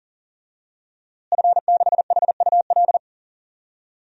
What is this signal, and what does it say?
Morse code, heard as a keyed tone, about 40 wpm, 700 Hz. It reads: F6HUL
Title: DXCC-100 Call Signs - 5 Characters - Repeated 2x - 40wpm